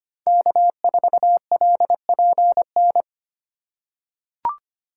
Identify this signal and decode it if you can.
Morse code, heard as a keyed tone, about 25 wpm, 700 Hz. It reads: K4LPN
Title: USA Call Signs - 5 Characters - 4x Character Spacing - 25wpm